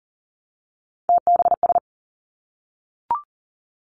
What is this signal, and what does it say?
Morse code, heard as a keyed tone, about 40 wpm, 700 Hz. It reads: TBS
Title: Three Challenging Letter-Number Combinations 40wpm